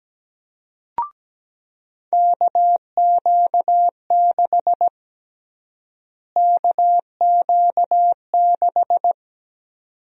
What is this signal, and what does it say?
Morse code, heard as a keyed tone, about 17 wpm, 700 Hz. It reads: KQ6 KQ6
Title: USA Call Sign Prefixes - Repeated 2x - 17wpm